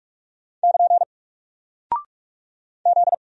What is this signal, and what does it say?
Morse code, heard as a keyed tone, about 46 wpm, 700 Hz. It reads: <KN> Z
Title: Single Character (Letter:Number:Punctuation:Prosign) Alt 45wpm